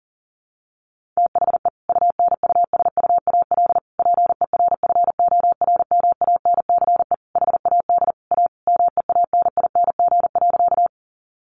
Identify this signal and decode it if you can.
Morse code, heard as a keyed tone, about 40 wpm, 700 Hz. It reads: THE UNUSUAL PERFORMANCE HAD A MEANING.
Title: Sentences from Top 1400 Words 40wpm